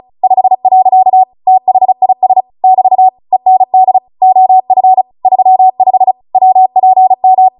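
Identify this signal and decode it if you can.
Morse code, heard as a keyed tone, about 36 wpm, 750 Hz. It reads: 5. THIS <BT> END OF 35 WPM
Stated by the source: Amateur radio Morse code practice archive transmission at 35 wpm.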